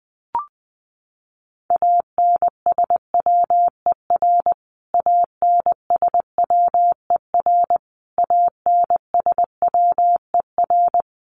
Morse code, heard as a keyed tone, about 20 wpm, 700 Hz. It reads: ANSWER ANSWER ANSWER
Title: Top 300 Words Repeated 20wpm